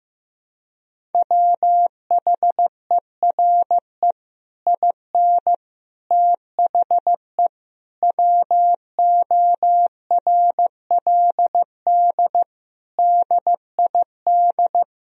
Morse code, heard as a keyed tone, about 15 wpm, 700 Hz. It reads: WHERE IN THE WORLD DID
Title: Sentences from Top 200 Words and Abbreviations 15wpm